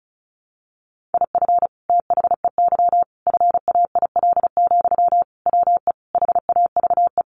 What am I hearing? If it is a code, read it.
Morse code, heard as a keyed tone, about 34 wpm, 700 Hz. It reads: IF THEY FAIL, WE HAVE
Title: Sentences from Top 1100 Words 35wpm